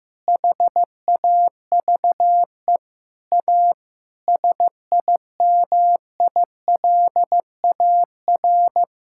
Morse code, heard as a keyed tone, about 15 wpm, 700 Hz. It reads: HAVE A SIMILAR
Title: Sentences from Top 1000 Words - Review Part 5 - 15wpm